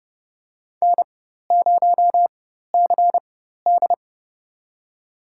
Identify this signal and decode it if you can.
Morse code, heard as a keyed tone, about 30 wpm, spaced as for 14 wpm, 700 Hz. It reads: N0CD
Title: USA Call Signs - 4 Characters - 4x Character Spacing - 30wpm